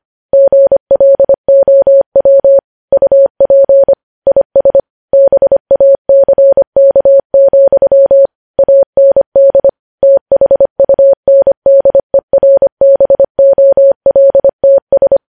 Morse code, heard as a keyed tone, about 25 wpm, 550 Hz. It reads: GLOW UP IS BACK, AND THUNDERBOLTS
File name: podcast / News headlines-1x-25x25